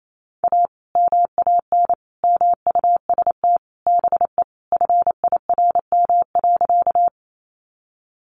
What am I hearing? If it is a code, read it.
Morse code, heard as a keyed tone, about 28 wpm, 700 Hz. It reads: A MAN MUST BE FIRM.
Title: Sentences from Top 1400 Words 28wpm